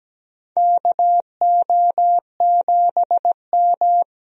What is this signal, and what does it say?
Morse code, heard as a keyed tone, about 17 wpm, 700 Hz. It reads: KO7M
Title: USA Call Signs - 4 Characters - 17wpm